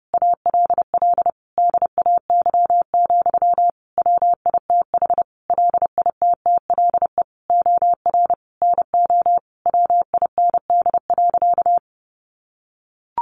Morse code, heard as a keyed tone, about 30 wpm, 700 Hz. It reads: ALL DAY, WITH LITTLE OR NO WIND.
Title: Sentences from Top 1400 Words 30wpm